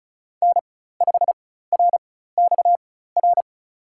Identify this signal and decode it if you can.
Morse code, heard as a keyed tone, about 36 wpm, spaced as for 16 wpm, 700 Hz. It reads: N5RXR
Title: USA Call Signs - 4x Character Spacing - 35wpm